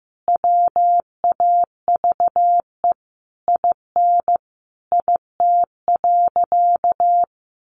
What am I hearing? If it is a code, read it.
Morse code, heard as a keyed tone, about 15 wpm, 700 Hz. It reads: WAVE IN IT.
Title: Sentences from Top 1400 Words 15wpm